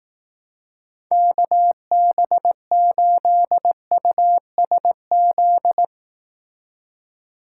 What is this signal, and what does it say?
Morse code, heard as a keyed tone, about 18 wpm, 700 Hz. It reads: KB8USZ
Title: Speed Racing - USA Call Signs 15wpm